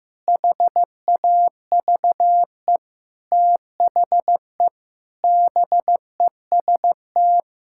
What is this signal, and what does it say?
Morse code, heard as a keyed tone, about 16 wpm, 700 Hz. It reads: HAVE THE BEST
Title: Sentences from Top 1000 Words - Review Part 6 - 15wpm